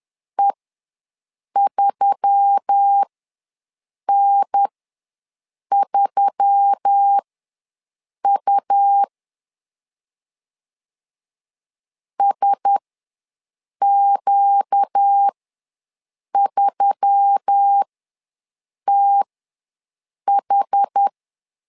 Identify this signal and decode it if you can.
Morse code, heard as a keyed tone, about 10 wpm, spaced as for 6 wpm, 800 Hz. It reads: E3N3U SQ3TH
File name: LEZIONI / 4 Q1S3 ws 10wpm 2437